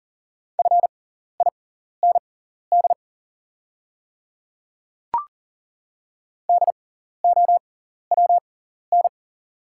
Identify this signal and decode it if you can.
Morse code, heard as a keyed tone, about 40 wpm, spaced as for 14 wpm, 690 Hz. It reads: FIND DOWN
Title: 6x Character Spacing - Top 100 Words - 40wpm